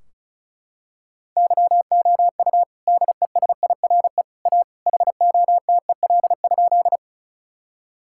Morse code, heard as a keyed tone, about 34 wpm, 700 Hz. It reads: YOU DESIRE A HOTEL?
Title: Sentences from Top 1600 Words 35wpm